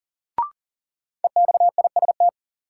Morse code, heard as a keyed tone, about 40 wpm, 700 Hz. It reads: EXIST
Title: Sentences from Top 1100 Words - New Word Drill Alt - 40wpm